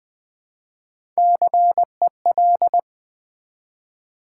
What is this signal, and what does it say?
Morse code, heard as a keyed tone, about 20 wpm, 700 Hz. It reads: CEL
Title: Build A Word - 1 Letter At A Time - 3 to 5-Letter Words - 20wpm